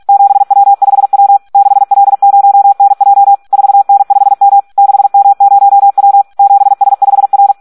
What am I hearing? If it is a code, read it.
Morse code, heard as a keyed tone, about 46 wpm, 800 Hz. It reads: 7W5W 6L0NJ 4N5M 6M0W 7S5R
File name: Audio practica CW / SD 1 46